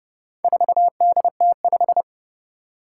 Morse code, heard as a keyed tone, about 30 wpm, 700 Hz. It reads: 4DT5
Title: Four Challenging Letter-Number Combinations 30wpm